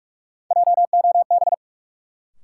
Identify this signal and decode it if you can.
Morse code, heard as a keyed tone, about 46 wpm, 700 Hz. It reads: JOB
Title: Sentences from Top 600 Words - New Word Drill - 45wpm